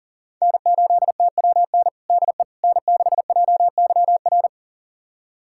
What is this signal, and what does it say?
Morse code, heard as a keyed tone, about 40 wpm, 700 Hz. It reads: N8TWN DE N6JYR
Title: USA Callsign DE Callsign 40wpm